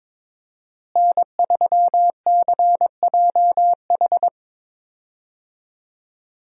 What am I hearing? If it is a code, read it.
Morse code, heard as a keyed tone, about 22 wpm, 700 Hz. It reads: N3CJH
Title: USA Call Signs - 5 Characters - Repeated 2x - 22wpm